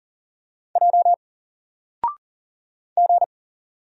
Morse code, heard as a keyed tone, about 40 wpm, 700 Hz. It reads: J G
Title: Single Character (Letter:Number:Punctuation:Prosign) Alt 40wpm